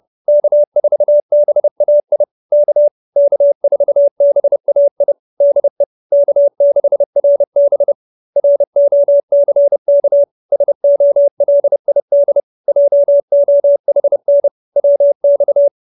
Morse code, heard as a keyed tone, about 30 wpm, 570 Hz. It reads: K4BAI K K4BAI DE K6RB ROCK SOLID JOHN WX